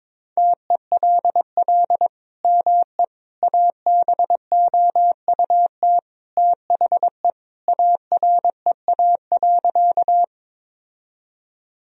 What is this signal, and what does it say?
Morse code, heard as a keyed tone, about 21 wpm, 700 Hz. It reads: TELL ME ABOUT THE AREA.
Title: Sentences from Top 1000 Words - Review Part 2 - 22wpm